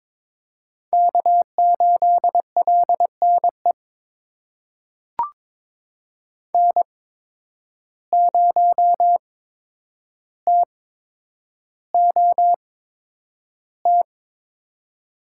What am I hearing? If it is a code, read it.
Morse code, heard as a keyed tone, about 22 wpm, spaced as for 9 wpm, 700 Hz. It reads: K8LNE N0TOT
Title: USA Call Signs - 5 Characters - 8x Character Spacing - 22wpm